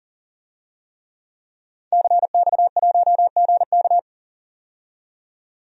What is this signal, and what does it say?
Morse code, heard as a keyed tone, about 41 wpm, 700 Hz. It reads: CX1GK
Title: DXCC-100 Call Signs - 5 Characters - 40wpm